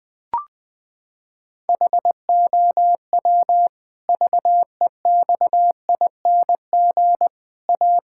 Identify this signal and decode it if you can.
Morse code, heard as a keyed tone, about 20 wpm, 700 Hz. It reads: HOW VEXING A
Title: Pangram Challenge v2 - 20wpm